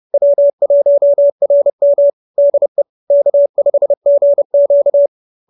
Morse code, heard as a keyed tone, about 30 wpm, 590 Hz. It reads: W1RM DE K5GQ
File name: QSO308 30